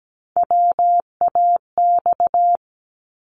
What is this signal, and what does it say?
Morse code, heard as a keyed tone, about 17 wpm, 700 Hz. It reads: WAX
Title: Multi-letter Phonemes 17wpm